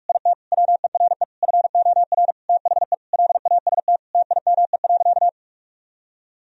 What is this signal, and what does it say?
Morse code, heard as a keyed tone, about 46 wpm, 700 Hz. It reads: IT WERE FOR THE LAST TIME.
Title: Sentences from Top 1000 Words - Review Part 2 - 45wpm